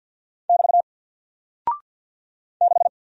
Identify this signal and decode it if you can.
Morse code, heard as a keyed tone, about 48 wpm, 700 Hz. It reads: <BT> 6
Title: Single Character (Letter:Number:Punctuation:Prosign) Alt 50wpm